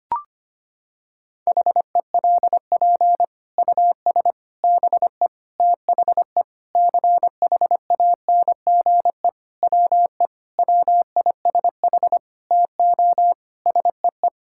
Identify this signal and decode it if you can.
Morse code, heard as a keyed tone, about 25 wpm, 700 Hz. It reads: HELP US BE THE CHANGE WE WISH TO SEE
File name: Sets of 9 Words - 25wpm